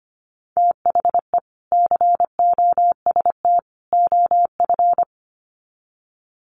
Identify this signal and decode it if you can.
Morse code, heard as a keyed tone, about 25 wpm, 700 Hz. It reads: THE COST OF
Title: Sets of 3 Words 25wpm